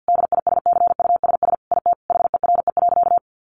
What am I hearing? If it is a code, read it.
Morse code, heard as a keyed tone, about 50 wpm, 700 Hz. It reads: DISCUSS IT HERE.
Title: Sentences from Top 1100 Words 50wpm